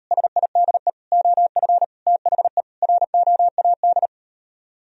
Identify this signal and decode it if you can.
Morse code, heard as a keyed tone, about 38 wpm, 700 Hz. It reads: SIDE OF THE ROAD